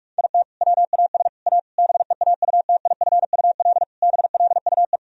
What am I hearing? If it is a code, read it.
Morse code, heard as a keyed tone, about 48 wpm, 700 Hz. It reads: IT WAS A BEAUTIFUL BLUE
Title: Sentences from Top 1000 Words - Review Part 3 - 45wpm